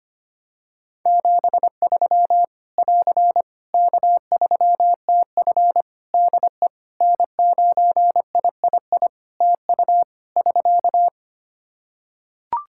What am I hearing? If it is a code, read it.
Morse code, heard as a keyed tone, about 25 wpm, 700 Hz. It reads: 73 <AR> K3TF DE N9III TU <SK>
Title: Practice QSO - Exchange 4 - Ending - 25wpm